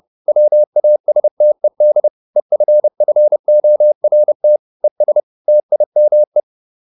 Morse code, heard as a keyed tone, about 30 wpm, 600 Hz. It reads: WASTED EFFORT ES TIME